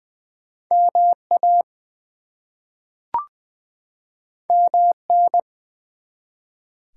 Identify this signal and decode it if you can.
Morse code, heard as a keyed tone, about 20 wpm, 700 Hz. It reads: MA MN
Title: US States and Canadian Provinces 20wpm